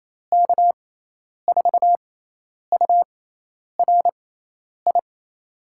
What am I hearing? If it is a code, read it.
Morse code, heard as a keyed tone, about 28 wpm, spaced as for 10 wpm, 700 Hz. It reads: K4URI
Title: USA Call Signs - 6x Character Spacing - 28wpm